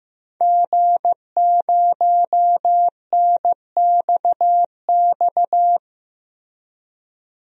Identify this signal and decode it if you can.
Morse code, heard as a keyed tone, about 15 wpm, 700 Hz. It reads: G0NXX
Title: DXCC-100 Call Signs - 5 Characters - Repeated 2x - 15wpm